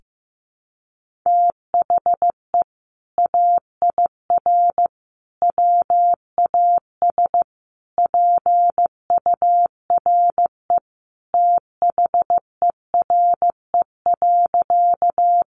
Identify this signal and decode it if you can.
Morse code, heard as a keyed tone, about 14 wpm, 700 Hz. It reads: THE AIR WAS PURE THERE.
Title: Sentences from Top 1500 Words 15wpm